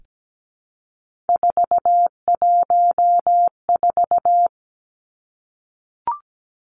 Morse code, heard as a keyed tone, about 17 wpm, 700 Hz. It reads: 414
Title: Numbers - 3-Digit 17wpm